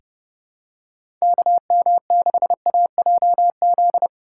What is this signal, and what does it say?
Morse code, heard as a keyed tone, about 30 wpm, 700 Hz. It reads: KM6AJZ
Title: USA Call Signs - 6 Characters - 4x Character Spacing - 30wpm